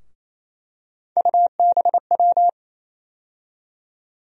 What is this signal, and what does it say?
Morse code, heard as a keyed tone, about 28 wpm, 700 Hz. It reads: UBW
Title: USA Call Sign Postfixes - 3 Letters 28wpm